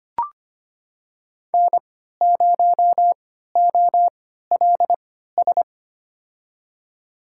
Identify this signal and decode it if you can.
Morse code, heard as a keyed tone, about 25 wpm, spaced as for 14 wpm, 700 Hz. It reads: N0OLS
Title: USA Call Signs - 5 Characters - 3x Character Spacing - 25wpm